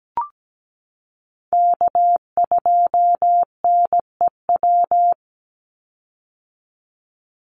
Morse code, heard as a keyed tone, about 17 wpm, 700 Hz. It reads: K2NEW
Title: USA Call Signs 17wpm